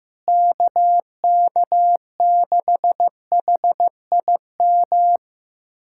Morse code, heard as a keyed tone, about 16 wpm, 700 Hz. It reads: KK6HIM
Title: USA Callsign DE Callsign - 6 Characters - 15wpm